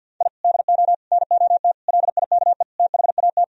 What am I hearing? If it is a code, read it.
Morse code, heard as a keyed tone, about 49 wpm, 700 Hz. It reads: I DO NOT LIKE THAT
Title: Sentences from Top 1000 Words - Review Part 2 - 50wpm